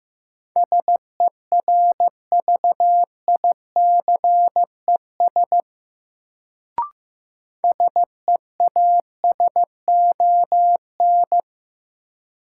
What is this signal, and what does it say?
Morse code, heard as a keyed tone, about 15 wpm, 700 Hz. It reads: SERVICES SEASON
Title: Sets of 5 Words - New Word Drill - 15wpm